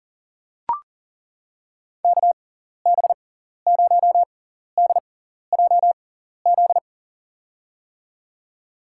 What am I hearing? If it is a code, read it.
Morse code, heard as a keyed tone, about 40 wpm, spaced as for 14 wpm, 700 Hz. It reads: KB0DJZ
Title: USA Call Signs - 6 Characters - 6x Character Spacing - 40wpm